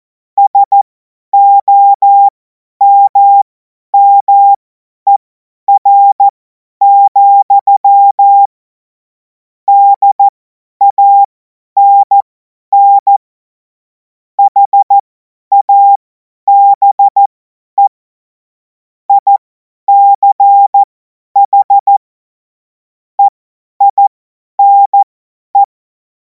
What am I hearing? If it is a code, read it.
Morse code, heard as a keyed tone, about 14 wpm, spaced as for 10 wpm, 800 Hz. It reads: SOMMER, DANN HABE ICH EINE
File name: Morsekurs / Witze / Witz 14. Juli 2026 50 F70